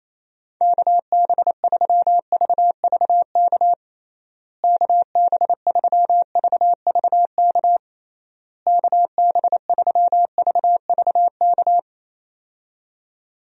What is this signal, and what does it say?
Morse code, heard as a keyed tone, about 28 wpm, 700 Hz. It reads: KB3VVK KB3VVK KB3VVK
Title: USA Call Signs - 6 Characters - Repeated 3x - 28wpm